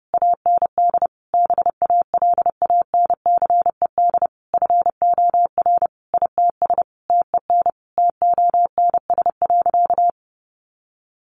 Morse code, heard as a keyed tone, about 30 wpm, 700 Hz. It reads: AND BALANCED FOR ITS TEN TONS.
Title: A Sound Of Thunder 30wpm